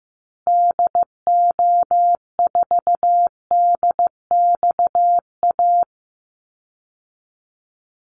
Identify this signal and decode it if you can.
Morse code, heard as a keyed tone, about 15 wpm, 700 Hz. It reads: DO4DXA
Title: Call Signs Repeated 3x - 15wpm